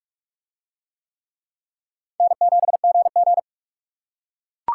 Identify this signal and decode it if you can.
Morse code, heard as a keyed tone, about 41 wpm, 700 Hz. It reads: N7GG